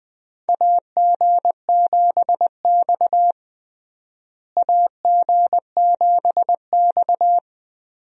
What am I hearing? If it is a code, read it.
Morse code, heard as a keyed tone, about 21 wpm, 700 Hz. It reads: AG7X AG7X
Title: USA Call Signs - 4 Characters - Repeated 3x - 20wpm